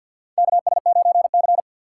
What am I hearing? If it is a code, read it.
Morse code, heard as a keyed tone, about 47 wpm, 700 Hz. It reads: KS9C
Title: USA Call Signs - 2x Character Spacing - 50wpm